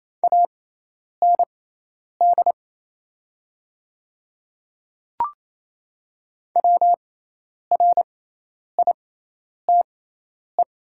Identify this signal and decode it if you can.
Morse code, heard as a keyed tone, about 28 wpm, spaced as for 10 wpm, 700 Hz. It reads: AND WRITE
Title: 6x Character Spacing - Top 100 Words - 28wpm